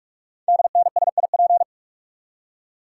Morse code, heard as a keyed tone, about 44 wpm, 700 Hz. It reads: DNSIP
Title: Code Groups - 5 Letters - 45wpm